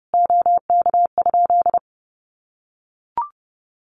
Morse code, heard as a keyed tone, about 30 wpm, 700 Hz. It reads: OK?
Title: Common QSO Phrases 30wpm